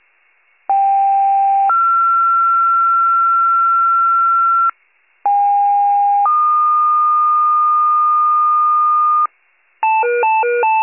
Two-tone page, heard as a one-second tone, then a three-second tone, about 790 then 1360 Hz, about 810 then 1220 Hz, with noise. It is heard from 0.7 to 9.3 s.
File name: pager / 73 RESCUE / wcalert